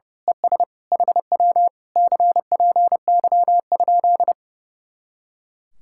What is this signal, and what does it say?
Morse code, heard as a keyed tone, about 28 wpm, 700 Hz. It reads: ES HW CPY?
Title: Speed-Racing Common QSO Phrases - 30wpm